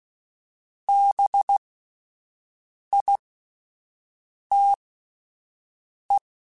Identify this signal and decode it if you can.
Morse code, heard as a keyed tone, about 16 wpm, spaced as for 5 wpm, 800 Hz. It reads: BITE
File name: Spaced Alphas / Spaced Alphas mp3s / Bs sp @10